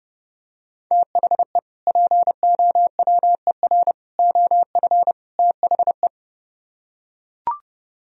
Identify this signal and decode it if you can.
Morse code, heard as a keyed tone, about 32 wpm, 700 Hz. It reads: THE POWER OF THE